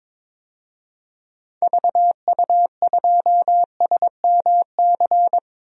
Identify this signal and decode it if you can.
Morse code, heard as a keyed tone, about 22 wpm, 700 Hz. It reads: VU2SMC
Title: DXCC-100 Call Signs - 6 Characters - 22wpm